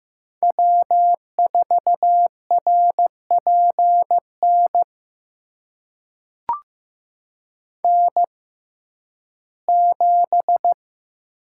Morse code, heard as a keyed tone, about 15 wpm, spaced as for 11 wpm, 700 Hz. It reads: W4RPN N7
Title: USA Call Signs - 5 Characters - 6x Character Spacing - 15wpm